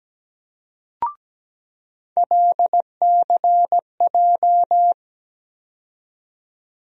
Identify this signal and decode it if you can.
Morse code, heard as a keyed tone, about 17 wpm, 700 Hz. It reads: LCJ
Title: USA Call Sign Postfixes - 3 Letters 17wpm